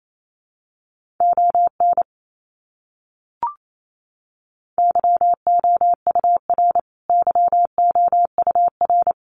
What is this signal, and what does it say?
Morse code, heard as a keyed tone, about 28 wpm, 700 Hz. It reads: ON YOUR YOUR
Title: Top 200 Words Repeated 28wpm